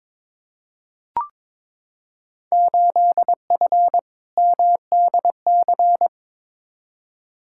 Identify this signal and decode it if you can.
Morse code, heard as a keyed tone, about 22 wpm, 700 Hz. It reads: 8F MDC
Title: Field Day Exchanges Encore - 22wpm